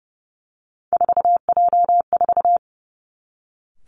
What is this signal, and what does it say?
Morse code, heard as a keyed tone, about 30 wpm, 700 Hz. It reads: 4J4
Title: Three Challenging Letter-Number Combinations 30wpm